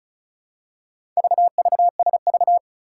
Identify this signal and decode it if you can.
Morse code, heard as a keyed tone, about 36 wpm, 700 Hz. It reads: VVSV
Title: Four Challenging Letter-Number Combinations 35wpm